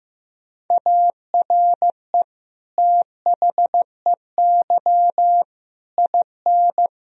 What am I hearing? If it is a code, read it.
Morse code, heard as a keyed tone, about 16 wpm, 700 Hz. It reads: ARE THEY IN
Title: Sentences from Top 1000 Words - Review Part 1 - 15wpm